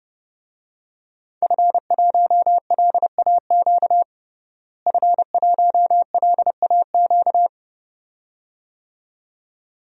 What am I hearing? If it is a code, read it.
Morse code, heard as a keyed tone, about 30 wpm, 700 Hz. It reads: F1LAQ F1LAQ
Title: DXCC-100 Call Signs - 5 Characters - Repeated 2x - 30wpm